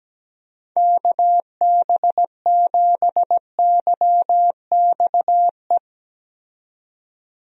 Morse code, heard as a keyed tone, about 17 wpm, 700 Hz. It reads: KB7YXE
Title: USA Call Signs - 6 Characters - 17wpm